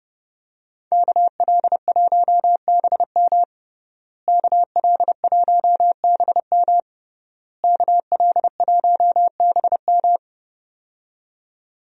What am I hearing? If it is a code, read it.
Morse code, heard as a keyed tone, about 30 wpm, 700 Hz. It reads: KL1BM KL1BM KL1BM
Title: USA Call Signs - 5 Characters - Repeated 3x - 30wpm